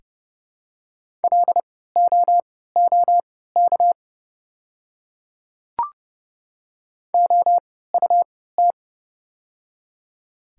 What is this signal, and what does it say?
Morse code, heard as a keyed tone, about 29 wpm, spaced as for 17 wpm, 700 Hz. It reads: LOOK OUT
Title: 3x Character Spacing - Top 100 Words - 30wpm